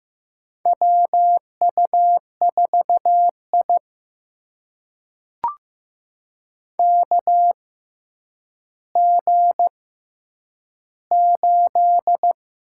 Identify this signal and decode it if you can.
Morse code, heard as a keyed tone, about 15 wpm, spaced as for 9 wpm, 700 Hz. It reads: WU4I KG8
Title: USA Call Signs - 4 Characters - 6x Character Spacing - 15wpm